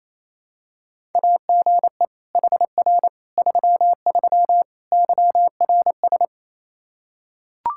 Morse code, heard as a keyed tone, about 28 wpm, 700 Hz. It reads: AGE HR 33 YRS
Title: Practice QSO - Exchange 3 - More Info - 28wpm